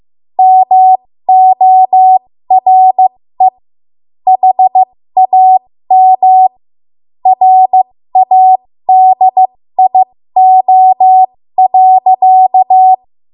Morse code, heard as a keyed tone, about 15 wpm, spaced as for 13 wpm, 750 Hz. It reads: MORE HAM RADIO.
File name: Archive / 13 WPM / 13WPM